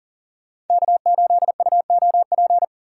Morse code, heard as a keyed tone, about 49 wpm, 700 Hz. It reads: K8UOP
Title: Speed Racing - USA 1x3 Call Signs 40wpm